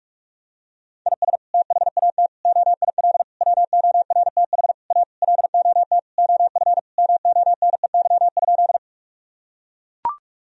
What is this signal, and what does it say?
Morse code, heard as a keyed tone, about 41 wpm, 700 Hz. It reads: IS THAT OIL WORTH A LOT OF MONEY?
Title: Sentences from Top 1000 Words - Review Part 4 - 45wpm